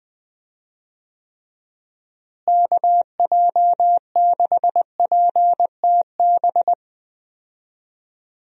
Morse code, heard as a keyed tone, about 20 wpm, 700 Hz. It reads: KJ6PTB
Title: USA Call Signs - 6 Characters - Repeated 3x - 20wpm